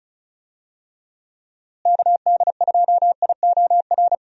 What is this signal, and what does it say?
Morse code, heard as a keyed tone, about 36 wpm, 700 Hz. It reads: KD2IOR
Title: USA Call Signs - 6 Characters - Repeated 2x - 35wpm